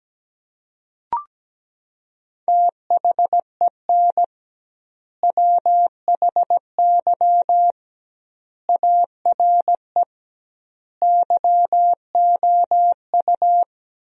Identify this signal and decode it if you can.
Morse code, heard as a keyed tone, about 17 wpm, 700 Hz. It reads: THEN WHY ARE YOU
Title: 2x Word-Spacing - Sentences from Top 200 Words - 17wpm